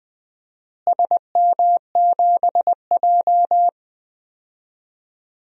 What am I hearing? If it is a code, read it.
Morse code, heard as a keyed tone, about 20 wpm, 700 Hz. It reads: SM7J
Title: DXCC-100 Call Signs - 4 Characters - Repeated 2x - 20wpm